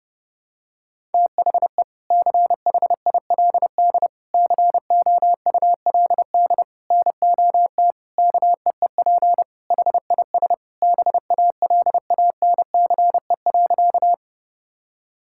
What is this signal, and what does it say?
Morse code, heard as a keyed tone, about 32 wpm, 700 Hz. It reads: THE CHILD COULD NOT KEEP HIS BALANCE.
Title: Sentences from Top 1000 Words - Review Part 6 - 30wpm